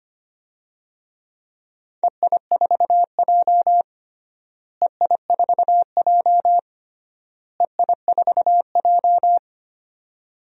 Morse code, heard as a keyed tone, about 25 wpm, 700 Hz. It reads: EI4J EI4J EI4J
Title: DXCC-100 Call Signs - 4 Characters - Repeated 3x - 25wpm